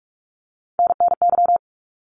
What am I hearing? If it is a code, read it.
Morse code, heard as a keyed tone, about 44 wpm, 700 Hz. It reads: NNY
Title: ARRL Sections 45wpm